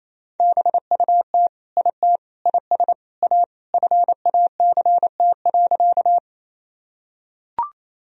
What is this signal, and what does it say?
Morse code, heard as a keyed tone, about 28 wpm, 700 Hz. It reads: BUT IT IS A FACT.
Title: Sentences from Top 1000 Words - Review Part 3 - 28wpm